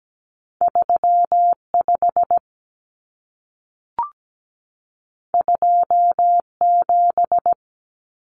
Morse code, heard as a keyed tone, about 17 wpm, 700 Hz. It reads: 3 5 2 7
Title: Common Cut Numbers - 2-Digit 17wpm